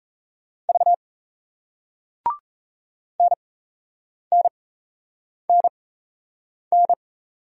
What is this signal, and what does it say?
Morse code, heard as a keyed tone, about 37 wpm, spaced as for 8 wpm, 700 Hz. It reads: V NNNN